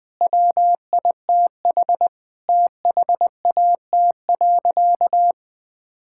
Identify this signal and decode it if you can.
Morse code, heard as a keyed tone, about 20 wpm, 700 Hz. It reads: WITH THAT.
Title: Sentences with 5-Word Phrases - 20wpm